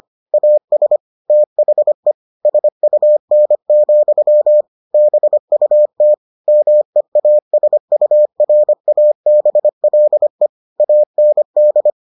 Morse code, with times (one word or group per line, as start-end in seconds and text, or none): 0.3-1.0 AS
1.3-2.1 THE
2.5-4.6 SUN,
4.9-6.1 BUT
6.5-10.5 MEASURABLE
10.8-11.9 AND